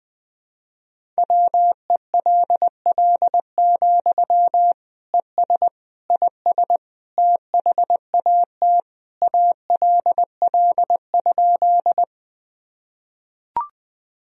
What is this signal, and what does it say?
Morse code, heard as a keyed tone, about 20 wpm, 700 Hz. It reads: WELL, ES IS THAT ALL?
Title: Sentences from Top 200 Words and Abbreviations 20wpm